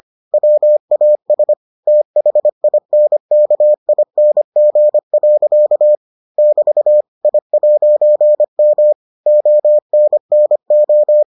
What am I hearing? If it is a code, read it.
Morse code, heard as a keyed tone, about 25 wpm, 600 Hz. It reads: WAS THINKING. <BT> I'M ONNO